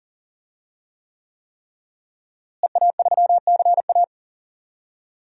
Morse code, heard as a keyed tone, about 40 wpm, 700 Hz. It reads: EA3CA
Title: DXCC-100 Call Signs - 5 Characters - Repeated 3x - 40wpm